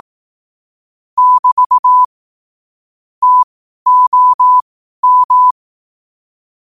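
Morse code, heard as a keyed tone, about 18 wpm, spaced as for 12 wpm, 1000 Hz. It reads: <BT> TOM
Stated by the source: Morse code quotes at 12 words per minute.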